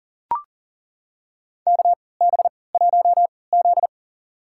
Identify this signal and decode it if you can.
Morse code, heard as a keyed tone, about 40 wpm, spaced as for 23 wpm, 700 Hz. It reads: KB1Z
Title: USA Call Signs - 4 Characters - 3x Character Spacing - 40wpm